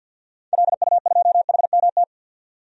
Morse code, heard as a keyed tone, about 50 wpm, 700 Hz. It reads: LU2HMT
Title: DXCC-100 Call Signs - 6 Characters - 50wpm